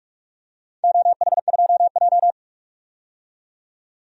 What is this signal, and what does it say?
Morse code, heard as a keyed tone, about 47 wpm, 700 Hz. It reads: OH2J
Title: DXCC-100 Call Signs - 4 Characters - 45wpm